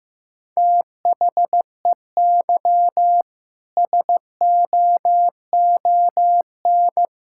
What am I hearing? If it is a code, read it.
Morse code, heard as a keyed tone, about 15 wpm, 700 Hz. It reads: THEY SOON
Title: Sentences from Top 1000 Words - Review Part 5 - 15wpm